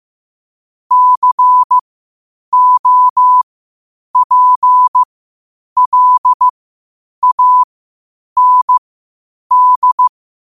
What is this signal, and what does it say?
Morse code, heard as a keyed tone, about 15 wpm, spaced as for 8 wpm, 1000 Hz. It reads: COPLAND
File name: qotdfiles / 8wpm